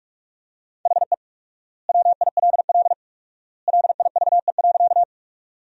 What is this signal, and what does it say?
Morse code, heard as a keyed tone, about 46 wpm, 700 Hz. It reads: HE WILL LIVE.
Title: 4x Word-Spacing - Sentences from Top 200 Words - 45wpm